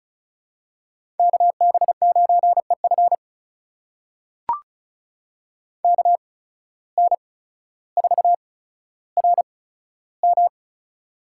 Morse code, heard as a keyed tone, about 36 wpm, spaced as for 15 wpm, 700 Hz. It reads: KB9EF KN4RM
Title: USA Call Signs - 5 Characters - 8x Character Spacing - 35wpm